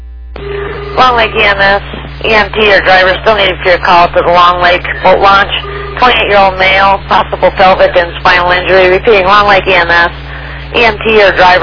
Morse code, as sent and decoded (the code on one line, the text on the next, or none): none